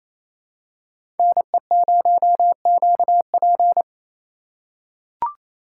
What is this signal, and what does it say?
Morse code, heard as a keyed tone, about 28 wpm, 700 Hz. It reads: NE0QP
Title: USA Call Signs - 5 Characters - 4x Character Spacing - 28wpm